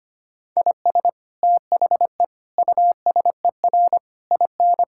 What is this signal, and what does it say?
Morse code, heard as a keyed tone, about 25 wpm, 700 Hz. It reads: IS THE USER IN
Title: Sentences from Top 1000 Words - Review Part 5 - 25wpm